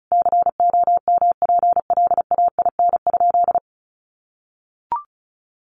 Morse code, heard as a keyed tone, about 34 wpm, 700 Hz. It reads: COMPLAIN?
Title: Sentences from Top 1300 Words 35wpm